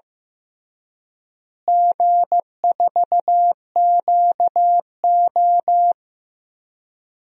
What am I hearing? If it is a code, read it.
Morse code, heard as a keyed tone, about 15 wpm, 700 Hz. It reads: G4QO
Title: DXCC-100 Call Signs - 4 Characters - Repeated 3x - 15wpm